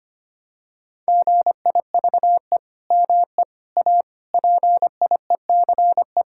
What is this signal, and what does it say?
Morse code, heard as a keyed tone, about 25 wpm, 700 Hz. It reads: GIVE ME A PIECE
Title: Sentences from Top 1600 Words 25wpm